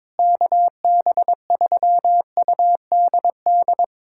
Morse code, heard as a keyed tone, about 21 wpm, 700 Hz. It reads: KB3UDD
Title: USA Callsign DE Callsign - 6 Characters - 22wpm